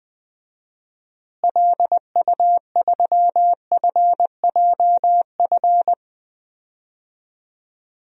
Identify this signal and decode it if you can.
Morse code, heard as a keyed tone, about 20 wpm, 700 Hz. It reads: LU3FJF
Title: DXCC-100 Call Signs - 6 Characters - Repeated 2x - 20wpm